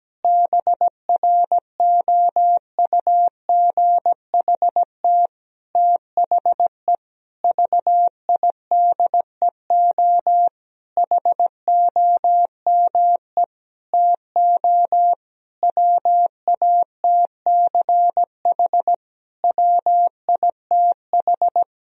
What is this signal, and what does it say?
Morse code, heard as a keyed tone, about 17 wpm, 700 Hz. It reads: BROUGHT THE VIDEO HOME TO WATCH WITH
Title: Sentences from Top 1000 Words - Review Part 4 - 17wpm